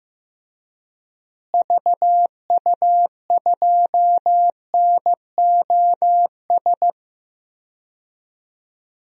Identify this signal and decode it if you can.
Morse code, heard as a keyed tone, about 15 wpm, 700 Hz. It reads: VU2NOS
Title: DXCC-100 Call Signs - 6 Characters - Repeated 2x - 15wpm